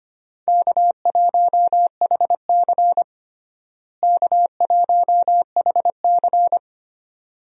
Morse code, heard as a keyed tone, about 25 wpm, 700 Hz. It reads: K1HC K1HC
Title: USA Call Signs - 4 Characters - Repeated 2x - 25wpm